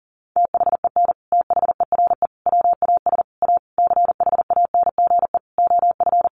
Morse code, heard as a keyed tone, about 40 wpm, 700 Hz. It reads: THEN THERE WAS A CHANGE OF
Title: Sentences from Top 1100 Words 40wpm